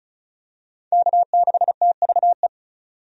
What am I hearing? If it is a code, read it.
Morse code, heard as a keyed tone, about 36 wpm, 700 Hz. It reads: K6TVE
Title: USA Call Signs - 5 Characters - 35wpm